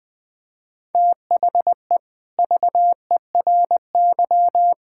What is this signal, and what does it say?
Morse code, heard as a keyed tone, about 20 wpm, 700 Hz. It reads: THE VERY
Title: Sentences from Top 1000 Words - Review Part 5 - 20wpm